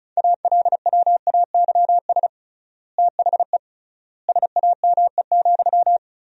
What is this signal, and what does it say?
Morse code, heard as a keyed tone, about 36 wpm, 700 Hz. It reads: ALWAYS THE SAME,
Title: 3x Word-Spacing - Sentences from Top 300 Words - 35wpm